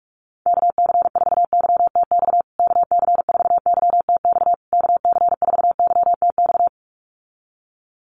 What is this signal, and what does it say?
Morse code, heard as a keyed tone, about 44 wpm, 700 Hz. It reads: KC4YTX KC4YTX KC4YTX
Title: USA Call Signs Repeated 3x 45wpm